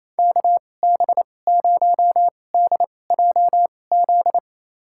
Morse code, heard as a keyed tone, about 28 wpm, spaced as for 20 wpm, 700 Hz. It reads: KB0DJZ
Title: USA Call Signs - 6 Characters - 2x Character Spacing - 28wpm